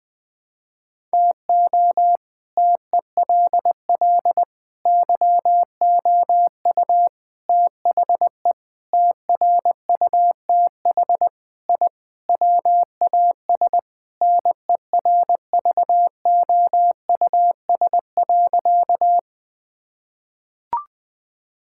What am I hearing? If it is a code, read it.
Morse code, heard as a keyed tone, about 20 wpm, 700 Hz. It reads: TO TELL YOU THE TRUTH I WAS NERVOUS.
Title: Sentences with 5-Word Phrases - 20wpm